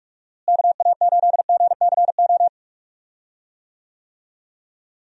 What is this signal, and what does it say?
Morse code, heard as a keyed tone, about 46 wpm, 700 Hz. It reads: KA8GCO
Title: USA Call Signs - 6 Characters - Repeated 2x - 45wpm